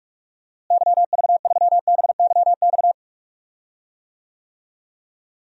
Morse code, heard as a keyed tone, about 42 wpm, 700 Hz. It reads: YV3BYX
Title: DXCC-100 Call Signs - 6 Characters - Repeated 2x - 45wpm